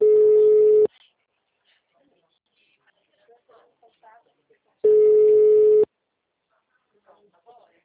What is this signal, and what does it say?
Ringback tone (Europe, 425 Hz), with speech.